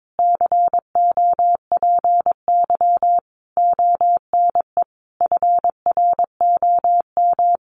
Morse code, heard as a keyed tone, about 22 wpm, 700 Hz. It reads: COPY ONE FROM
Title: Sentences from Top 1400 Words 22wpm